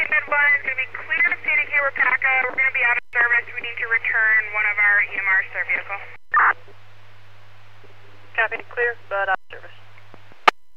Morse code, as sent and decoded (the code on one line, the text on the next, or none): none